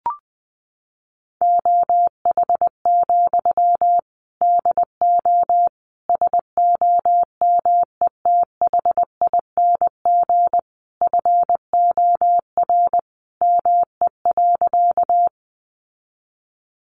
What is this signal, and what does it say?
Morse code, heard as a keyed tone, about 20 wpm, 700 Hz. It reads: OH, DO SOMETHING FOR ME.
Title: Sentences from Top 600 Words 20wpm